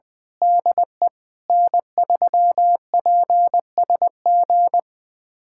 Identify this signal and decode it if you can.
Morse code, heard as a keyed tone, about 20 wpm, 700 Hz. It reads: DE N3PSG
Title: USA Callsign DE Callsign - 5 Characters - 20wpm